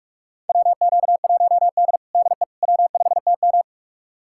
Morse code, heard as a keyed tone, about 42 wpm, 700 Hz. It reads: WQ1D DE W5TM
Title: USA Callsign DE Callsign - 4 Characters - 45wpm